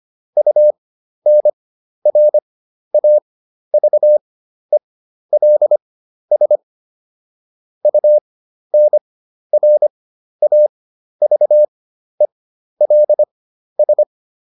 Morse code, heard as a keyed tone, about 25 wpm, spaced as for 12 wpm, 600 Hz. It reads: UNRAVELS UNRAVELS